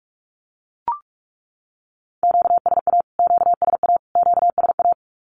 Morse code, heard as a keyed tone, about 44 wpm, 700 Hz. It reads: QSA QSA QSA
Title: Q-Codes Repeated 3x 45wpm